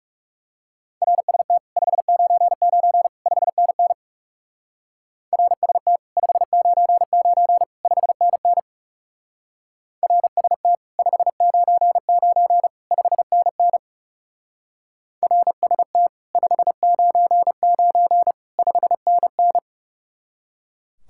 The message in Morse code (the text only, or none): RST 599 5NN RST 599 5NN RST 599 5NN RST 599 5NN